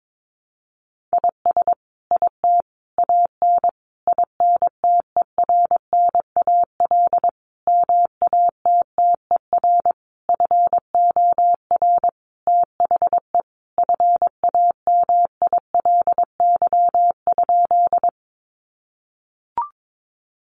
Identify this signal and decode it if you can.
Morse code, heard as a keyed tone, about 22 wpm, 700 Hz. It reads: IS IT AN INTERNAL MATTER FOR THE FAMILY?
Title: Sentences from Top 1400 Words 22wpm